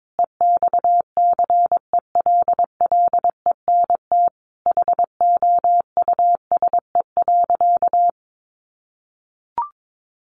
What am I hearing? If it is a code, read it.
Morse code, heard as a keyed tone, about 22 wpm, 700 Hz. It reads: EXCELLENT HOUSE.
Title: Sentences from Top 1000 Words 22wpm